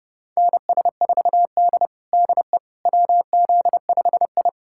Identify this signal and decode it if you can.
Morse code, heard as a keyed tone, about 30 wpm, 700 Hz. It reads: NS4D DE WZ5I
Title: USA Callsign DE Callsign - 4 Characters - 30wpm